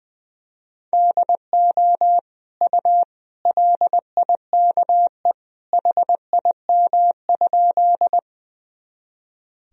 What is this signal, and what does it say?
Morse code, heard as a keyed tone, about 20 wpm, 700 Hz. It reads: DO U LIKE HIM?
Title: Sentences from Top 100 Words and Abbreviations 20wpm